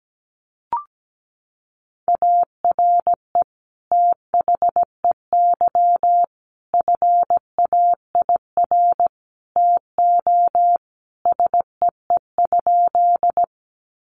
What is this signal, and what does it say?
Morse code, heard as a keyed tone, about 17 wpm, 700 Hz. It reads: ARE THEY FAIR TO SEE?
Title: Sentences from Top 1300 Words 17wpm